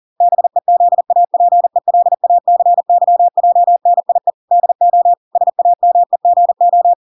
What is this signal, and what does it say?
Morse code, heard as a keyed tone, about 40 wpm, 700 Hz. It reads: BEZAPELACYJNIE DO SAMEGO